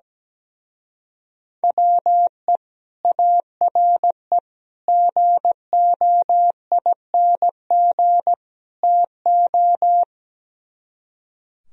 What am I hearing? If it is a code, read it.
Morse code, heard as a keyed tone, about 17 wpm, 700 Hz. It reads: WE ARE GOING TO
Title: Speed-Racing - Sets of 4 Words 17wpm